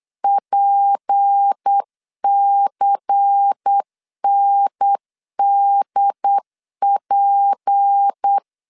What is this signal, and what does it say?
Morse code, heard as a keyed tone, about 8 wpm, 800 Hz. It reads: PCNDP
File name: LEZIONI / 6 PKX0 / 6 P 8wpm 3025